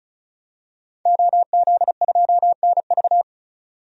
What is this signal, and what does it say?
Morse code, heard as a keyed tone, about 34 wpm, 700 Hz. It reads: OZ2NV
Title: DXCC-100 Call Signs - 5 Characters - 35wpm